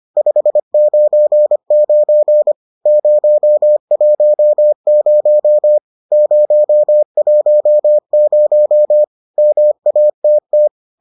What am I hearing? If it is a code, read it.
Morse code, heard as a keyed tone, about 25 wpm, 600 Hz. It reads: 599 010 010 MATT